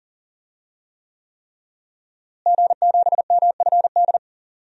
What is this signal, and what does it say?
Morse code, heard as a keyed tone, about 40 wpm, 700 Hz. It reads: G7MFD
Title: DXCC-100 Call Signs - 5 Characters - Repeated 3x - 40wpm